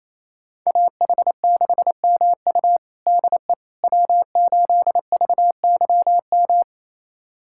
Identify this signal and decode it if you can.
Morse code, heard as a keyed tone, about 28 wpm, 700 Hz. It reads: AH6MU DE W8VYM
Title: USA Callsign DE Callsign - 5 Characters - 28wpm